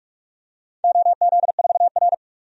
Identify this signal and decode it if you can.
Morse code, heard as a keyed tone, about 44 wpm, 690 Hz. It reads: OZ4R
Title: DXCC-100 Call Signs - 4 Characters - 45wpm